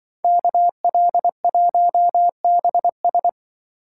Morse code, heard as a keyed tone, about 24 wpm, 700 Hz. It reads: KL1BS
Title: Speed Racing - USA 2x2 Call Signs 20wpm